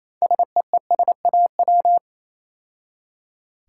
Morse code, heard as a keyed tone, about 27 wpm, 700 Hz. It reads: SEESAW
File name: Easy Words - Speed-Racing with Hints - 22wpm